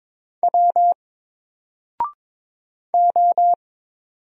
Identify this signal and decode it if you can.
Morse code, heard as a keyed tone, about 22 wpm, 700 Hz. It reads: W O
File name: Single-Letter-Number-22wpm